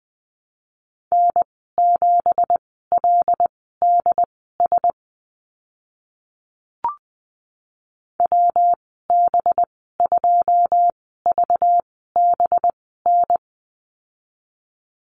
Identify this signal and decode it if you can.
Morse code, heard as a keyed tone, about 20 wpm, spaced as for 14 wpm, 700 Hz. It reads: N7LDS WB2VBN
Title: 193 Lesson 31g - Letter v - Call Signs 20wpm